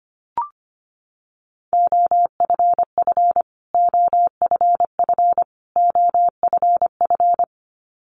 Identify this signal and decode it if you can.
Morse code, heard as a keyed tone, about 25 wpm, 700 Hz. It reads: OFF OFF OFF
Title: Top 500 Words Repeated 25wpm